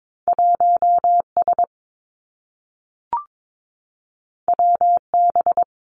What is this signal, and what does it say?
Morse code, heard as a keyed tone, about 22 wpm, 700 Hz. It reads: 1 S W B
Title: Two Challenging Letter-Number Combinations 22wpm